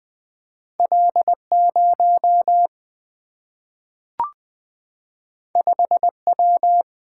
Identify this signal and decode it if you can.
Morse code, heard as a keyed tone, about 20 wpm, 700 Hz. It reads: L 0 5 W
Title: Single-Letter Single-Number 20wpm